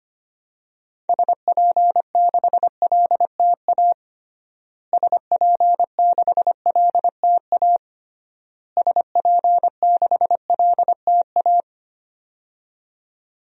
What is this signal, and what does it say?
Morse code, heard as a keyed tone, about 25 wpm, 700 Hz. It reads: SP6LTA SP6LTA SP6LTA
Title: DXCC-100 Call Signs - 6 Characters - Repeated 3x - 25wpm